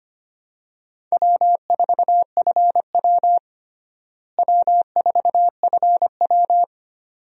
Morse code, heard as a keyed tone, about 25 wpm, 700 Hz. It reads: W4FW W4FW
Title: USA Call Signs - 4 Characters - Repeated 2x - 25wpm